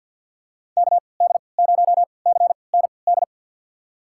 Morse code, heard as a keyed tone, about 53 wpm, spaced as for 29 wpm, 700 Hz. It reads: KD0CND
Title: USA Call Signs - 6 Characters - 3x Character Spacing - 50wpm